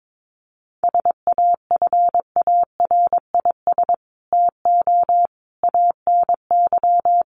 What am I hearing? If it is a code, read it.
Morse code, heard as a keyed tone, about 22 wpm, 700 Hz. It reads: SAFARIS TO ANY
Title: A Sound Of Thunder 22wpm